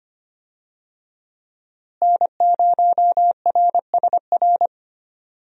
Morse code, heard as a keyed tone, about 25 wpm, 700 Hz. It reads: N0RSR
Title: USA Call Signs - 5 Characters - Repeated 2x - 25wpm